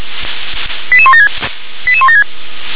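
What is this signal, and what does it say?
Five-tone selective call (ZVEI), heard as quick runs of five tones, tones of about 70 ms. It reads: 88176 88176
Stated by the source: ZVEI five-tone call received via SDR.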